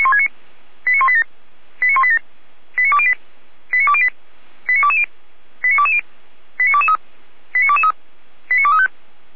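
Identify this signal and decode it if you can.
Five-tone selective call (ZVEI), heard as quick runs of five tones, tones of about 70 ms. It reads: 89187 89187 89208 89208 89220 89220 89223 89223 89235
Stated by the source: ZVEI five-tone call received by SDR, logged by Feuerwehr Orlamünde.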